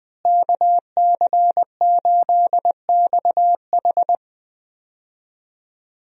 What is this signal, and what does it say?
Morse code, heard as a keyed tone, about 20 wpm, 700 Hz. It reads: KC8XH
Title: USA Call Signs - 5 Characters - Repeated 3x - 20wpm